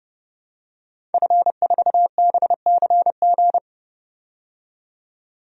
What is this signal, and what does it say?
Morse code, heard as a keyed tone, about 30 wpm, 700 Hz. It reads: F4BCG
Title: DXCC-100 Call Signs - 5 Characters - 30wpm